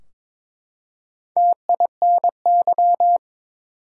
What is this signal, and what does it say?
Morse code, heard as a keyed tone, about 22 wpm, 700 Hz. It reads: TINY
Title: Four Letter Words - 22wpm